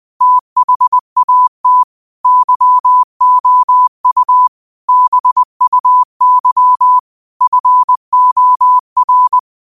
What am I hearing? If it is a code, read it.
Morse code, heard as a keyed tone, about 20 wpm, 1000 Hz. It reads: THAT YOU BUY FOR
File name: 20wpm